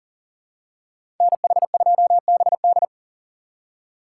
Morse code, heard as a keyed tone, about 40 wpm, 700 Hz. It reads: NH2BD
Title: USA Call Signs - 5 Characters - 4x Character Spacing - 40wpm